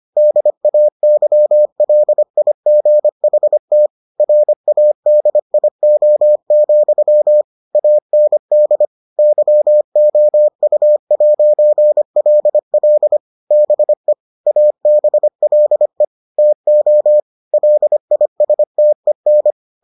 Morse code, heard as a keyed tone, about 25 wpm, 600 Hz. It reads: DAYLIGHT RADIO, AND YOU'LL BE ABLE TO LISTEN